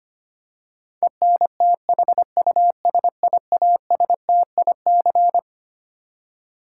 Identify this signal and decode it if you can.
Morse code, heard as a keyed tone, about 25 wpm, 700 Hz. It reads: ENTHUSIASTIC
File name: Twelve Letter Words - 25wpm